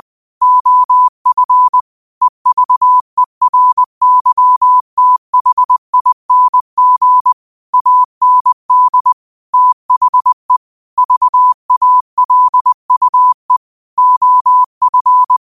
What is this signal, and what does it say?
Morse code, heard as a keyed tone, about 20 wpm, 1000 Hz. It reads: OF EVERYTHING AND THE VALUE OF
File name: qotdfiles / 20wpm